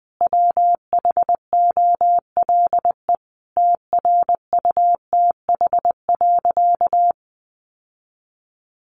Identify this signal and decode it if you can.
Morse code, heard as a keyed tone, about 20 wpm, 700 Hz. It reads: WHOLE TRUTH.
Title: Sentences from Top 800 Words 20wpm